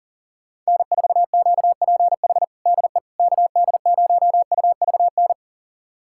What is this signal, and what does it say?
Morse code, heard as a keyed tone, about 39 wpm, 700 Hz. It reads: N4QPH DE KD0UVN
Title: USA Callsign DE Callsign 40wpm